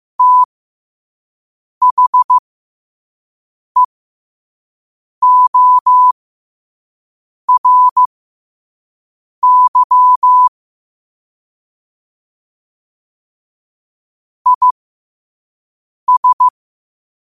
Morse code, heard as a keyed tone, about 15 wpm, spaced as for 5 wpm, 1000 Hz. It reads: THEORY IS
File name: qotdfiles / 5wpm